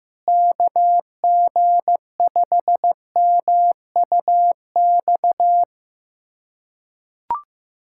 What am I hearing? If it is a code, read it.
Morse code, heard as a keyed tone, about 15 wpm, 700 Hz. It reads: KG5MUX
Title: USA Callsign DE Callsign - 6 Characters - 15wpm